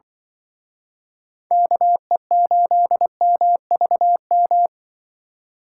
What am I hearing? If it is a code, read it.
Morse code, heard as a keyed tone, about 23 wpm, 700 Hz. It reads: KE8MVM
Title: Speed Racing - USA 2x3 Call Signs 20wpm